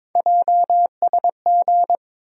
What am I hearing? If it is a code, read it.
Morse code, heard as a keyed tone, about 22 wpm, 700 Hz. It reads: JSG
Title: USA Call Sign Postfixes - 3 Letters 22wpm